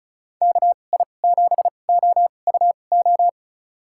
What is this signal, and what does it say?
Morse code, heard as a keyed tone, about 34 wpm, spaced as for 25 wpm, 700 Hz. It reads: KI7OUO
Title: USA Call Signs - 6 Characters - 2x Character Spacing - 35wpm